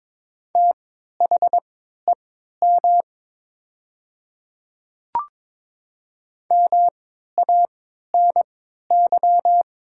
Morse code, heard as a keyed tone, about 22 wpm, spaced as for 12 wpm, 700 Hz. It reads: THEM MANY
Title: 3x Character Spacing - Top 100 Words - 22wpm